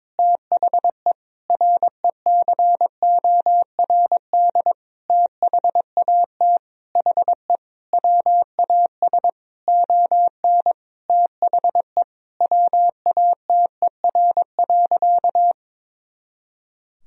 Morse code, heard as a keyed tone, about 22 wpm, 700 Hz. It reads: THE RECORD THAT HE WAS ON THE WATER.
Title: Sentences from Top 500 Words and Abbreviations 22wpm